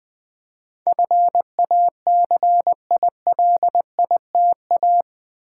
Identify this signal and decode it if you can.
Morse code, heard as a keyed tone, about 20 wpm, 700 Hz. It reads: FACILITA
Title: Build A Word - 4 Letters At A Time - 10-Letter Words - 20wpm